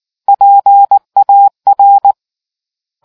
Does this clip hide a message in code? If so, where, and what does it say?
Yes: Morse code from 0.3 to 2.1 s, heard as a keyed tone, about 19 wpm, 800 Hz. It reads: PAR